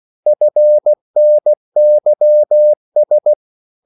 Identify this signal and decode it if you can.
Morse code, heard as a keyed tone, about 16 wpm, 600 Hz. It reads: FNYS
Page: Korte danske ord 16wpm